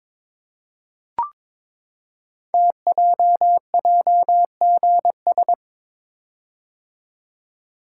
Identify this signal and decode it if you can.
Morse code, heard as a keyed tone, about 22 wpm, 700 Hz. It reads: TJJGS
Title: Code Groups - 5 Letters - 22wpm